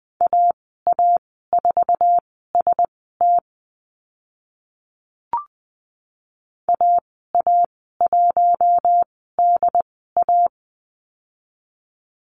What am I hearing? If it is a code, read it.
Morse code, heard as a keyed tone, about 20 wpm, spaced as for 14 wpm, 700 Hz. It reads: AA4ST AA1DA
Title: 069 Lesson 12g - Letter d - Call Signs 20wpm